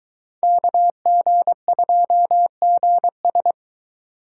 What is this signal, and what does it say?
Morse code, heard as a keyed tone, about 23 wpm, 700 Hz. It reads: KG2GS